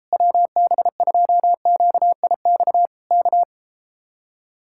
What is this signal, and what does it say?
Morse code, heard as a keyed tone, about 34 wpm, 700 Hz. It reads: WB2QIX K